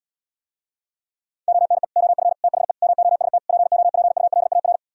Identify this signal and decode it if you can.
Morse code, heard as a keyed tone, about 77 wpm, 700 Hz. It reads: ONE OF THE MOST POPULAR